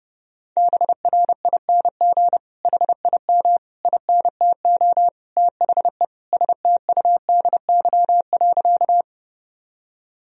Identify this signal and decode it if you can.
Morse code, heard as a keyed tone, about 30 wpm, 700 Hz. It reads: BRING HIM INTO THE STUDY.
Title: Sentences from Top 1000 Words - Review Part 1 - 30wpm